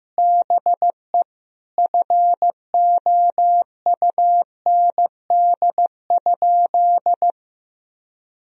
Morse code, heard as a keyed tone, about 15 wpm, 700 Hz. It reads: BE FOUND?
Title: Sentences from Top 300 Words and Abbreviations 15wpm